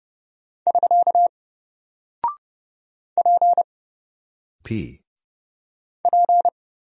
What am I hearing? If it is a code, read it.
Morse code, heard as a keyed tone, about 30 wpm, 700 Hz. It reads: <SK> P P
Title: Single Character (Letter:Number:Punctuation:Prosign) 30wpm